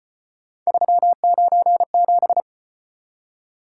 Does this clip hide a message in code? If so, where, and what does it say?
Yes: Morse code from 0.7 to 2.4 s, heard as a keyed tone, about 34 wpm, 700 Hz. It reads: 397